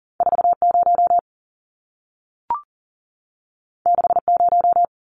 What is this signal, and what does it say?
Morse code, heard as a keyed tone, about 40 wpm, 700 Hz. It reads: 40 60
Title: Numbers - 2-Digit 40wpm